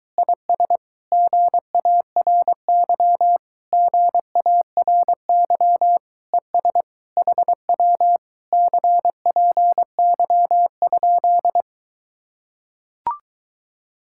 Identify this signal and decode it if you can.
Morse code, heard as a keyed tone, about 23 wpm, 700 Hz. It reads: IS GARY GARY ES HW CPY?